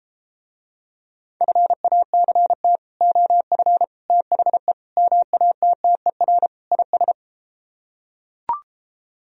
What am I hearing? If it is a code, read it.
Morse code, heard as a keyed tone, about 34 wpm, 700 Hz. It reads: FACT OF THE MATTER IS